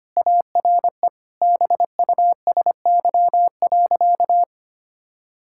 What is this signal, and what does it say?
Morse code, heard as a keyed tone, about 25 wpm, 700 Hz. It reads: ARE BUSY.
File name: practice / 25wpm / Sentences with 3-Word Phrases - 25wpm